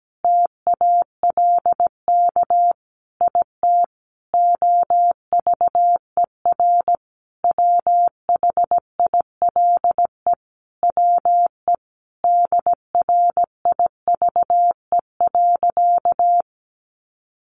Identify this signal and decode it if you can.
Morse code, heard as a keyed tone, about 17 wpm, 700 Hz. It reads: TALK IT OVER WHILE WE DRIVE.
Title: Sentences from Top 1300 Words 17wpm